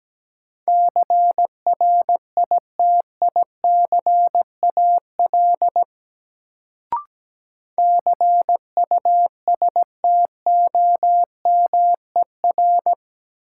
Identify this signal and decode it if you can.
Morse code, heard as a keyed tone, about 17 wpm, 700 Hz. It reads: CRITICAL CUSTOMER
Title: Sentences from Top 1100 Words - New Word Drill - 17wpm